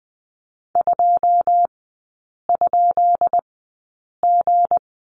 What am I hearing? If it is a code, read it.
Morse code, heard as a keyed tone, about 20 wpm, spaced as for 8 wpm, 700 Hz. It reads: 2?G
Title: 184 Lesson 30e - Letter b - Three Characters 20wpm